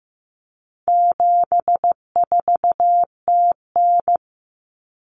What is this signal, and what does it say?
Morse code, heard as a keyed tone, about 15 wpm, 700 Hz. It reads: 74TN
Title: Common Cut Numbers - 4-Digit 15wpm